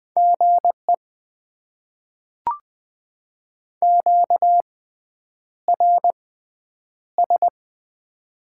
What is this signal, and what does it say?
Morse code, heard as a keyed tone, about 20 wpm, spaced as for 9 wpm, 700 Hz. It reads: GE QRS
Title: Top 90 QSO Elements - 6x Character Spacing - 20wpm